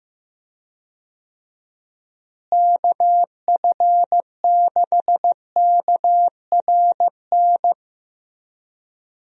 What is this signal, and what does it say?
Morse code, heard as a keyed tone, about 15 wpm, 700 Hz. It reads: KF6KRN
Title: USA Call Signs - 6 Characters - Repeated 2x - 15wpm